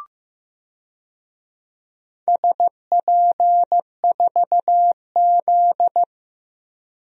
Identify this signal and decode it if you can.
Morse code, heard as a keyed tone, about 15 wpm, 700 Hz. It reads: SP4Z
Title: DXCC-100 Call Signs - 4 Characters - 15wpm